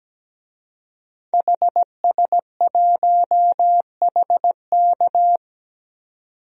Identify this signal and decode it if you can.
Morse code, heard as a keyed tone, about 17 wpm, 700 Hz. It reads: HS1HK
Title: DXCC-100 Call Signs - 5 Characters - 17wpm